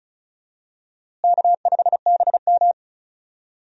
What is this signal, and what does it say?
Morse code, heard as a keyed tone, about 34 wpm, 700 Hz. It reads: K5BM
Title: USA Call Signs - 4 Characters - Repeated 3x - 35wpm